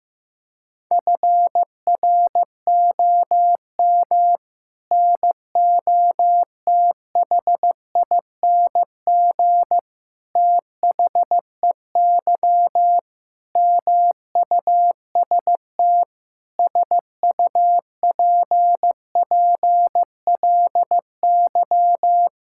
Morse code, heard as a keyed tone, about 15 wpm, 700 Hz. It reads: FROM NOTHING THEY MUST SUPPLY
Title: Sentences from Top 1000 Words - Review Part 5 - 15wpm